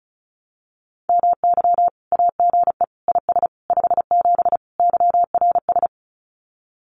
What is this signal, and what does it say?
Morse code, heard as a keyed tone, about 34 wpm, 700 Hz. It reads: MY AGE IS 57 YRS
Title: Common QSO Phrases Encore 35wpm